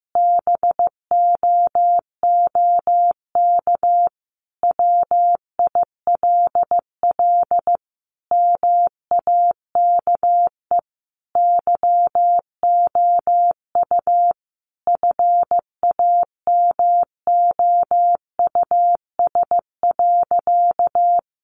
Morse code, heard as a keyed tone, about 15 wpm, 700 Hz. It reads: BOOK WILL MAKE YOU FAMOUS.
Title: Sentences from Top 1500 Words 15wpm